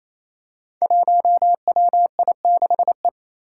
Morse code, heard as a keyed tone, about 28 wpm, 700 Hz. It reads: 1WI6E
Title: Five Challenging Letter-Number Combinations 28wpm